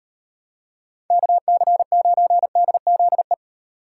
Morse code, heard as a keyed tone, about 37 wpm, 700 Hz. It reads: KC9DZE